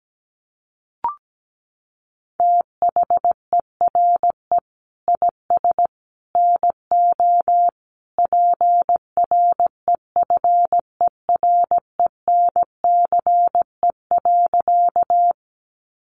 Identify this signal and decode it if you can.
Morse code, heard as a keyed tone, about 17 wpm, 700 Hz. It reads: THERE IS NO PREFERENCE.
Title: Sentences from Top 1500 Words 17wpm